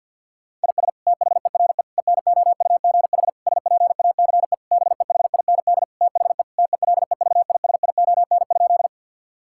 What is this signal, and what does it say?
Morse code, heard as a keyed tone, about 52 wpm, 690 Hz. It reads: IS THERE ENOUGH SPACE BEHIND THE TELEVISION?
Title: Sentences from Top 1000 Words - Review Part 5 - 50wpm